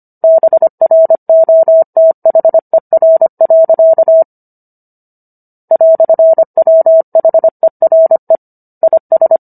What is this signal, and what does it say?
Morse code, heard as a keyed tone, about 25 wpm, 650 Hz. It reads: BROTHER. "WHERE IS